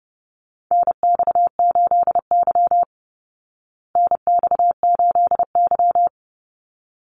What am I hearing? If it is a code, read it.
Morse code, heard as a keyed tone, about 30 wpm, 700 Hz. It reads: NX8Y NX8Y
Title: Call Signs Repeated 3x - 30wpm